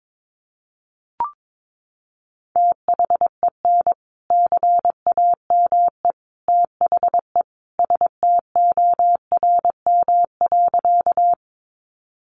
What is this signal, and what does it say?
Morse code, heard as a keyed tone, about 22 wpm, 700 Hz. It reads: THEN CAME THE STORM.
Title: Sentences from Top 1500 Words 22wpm